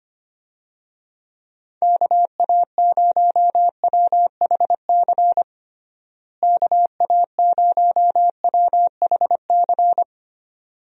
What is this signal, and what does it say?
Morse code, heard as a keyed tone, about 25 wpm, 700 Hz. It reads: KA0WHC KA0WHC
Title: USA Call Signs - 6 Characters - Repeated 2x - 25wpm